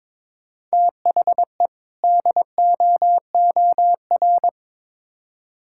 Morse code, heard as a keyed tone, about 21 wpm, 700 Hz. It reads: THE DOOR
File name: Build A Phrase - 2-Word Phrases - 22wpm